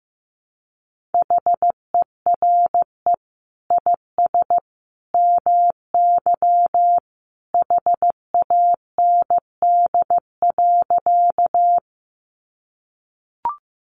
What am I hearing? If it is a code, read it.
Morse code, heard as a keyed tone, about 15 wpm, 700 Hz. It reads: HERE IS MY HAND.
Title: Sentences from Top 200 Words 15wpm